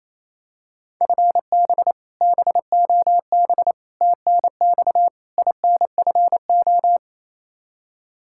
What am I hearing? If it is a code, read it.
Morse code, heard as a keyed tone, about 27 wpm, 700 Hz. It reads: FB BOB TNX INFO
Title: Practice QSO - Exchange 3 - More Info - 28wpm